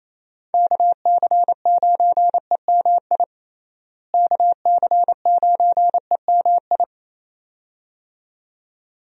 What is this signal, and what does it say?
Morse code, heard as a keyed tone, about 28 wpm, 700 Hz. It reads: KC9EMI KC9EMI
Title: USA Call Signs - 6 Characters - Repeated 2x - 28wpm